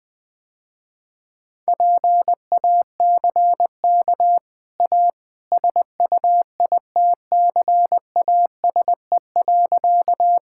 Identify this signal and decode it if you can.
Morse code, heard as a keyed tone, about 20 wpm, 700 Hz. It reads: PACK A SUITCASE.
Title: Easy Sentences with Hints - 20wpm